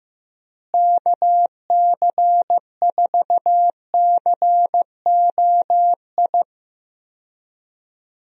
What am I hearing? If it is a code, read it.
Morse code, heard as a keyed tone, about 16 wpm, 700 Hz. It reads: KC4COI
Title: USA Call Signs - 6 Characters - Repeated 2x - 15wpm